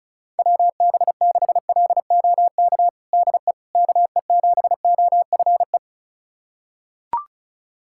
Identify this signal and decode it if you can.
Morse code, heard as a keyed tone, about 34 wpm, 700 Hz. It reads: WB6LOK DE KE7OFE
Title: USA Callsign DE Callsign - 6 Characters - 35wpm